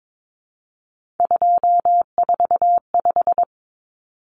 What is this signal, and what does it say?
Morse code, heard as a keyed tone, about 22 wpm, 700 Hz. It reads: 245
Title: Common Cut Numbers - 3-Digit 22wpm